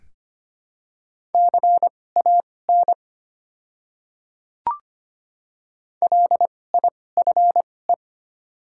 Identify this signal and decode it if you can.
Morse code, heard as a keyed tone, about 25 wpm, spaced as for 18 wpm, 700 Hz. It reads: CAN LIFE
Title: 2x Character Spacing - Top 500 Words - 25wpm